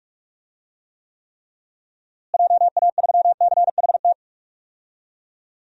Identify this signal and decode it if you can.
Morse code, heard as a keyed tone, about 41 wpm, 700 Hz. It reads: JA3CHT
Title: DXCC-100 Call Signs - 6 Characters - Repeated 3x - 45wpm